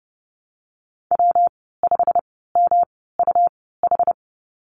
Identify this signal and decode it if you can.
Morse code, heard as a keyed tone, about 30 wpm, 700 Hz. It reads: W 5 M U H
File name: Lesson 19g - Letter w - Call Signs 30wpm